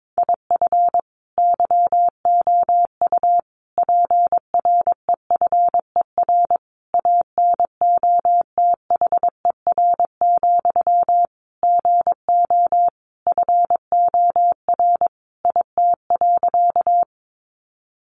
Morse code, heard as a keyed tone, about 22 wpm, 700 Hz. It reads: IF YOU PREFER ANOTHER, GO FOR IT.
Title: Sentences from Top 1100 Words 22wpm